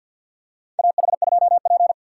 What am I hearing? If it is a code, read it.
Morse code, heard as a keyed tone, about 52 wpm, 700 Hz. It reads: AH2P
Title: USA Call Signs - 6x Character Spacing - 50wpm